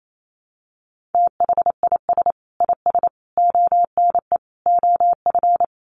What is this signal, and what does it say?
Morse code, heard as a keyed tone, about 28 wpm, 700 Hz. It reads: THIS IS ONE OF
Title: Sets of 4 Words 28wpm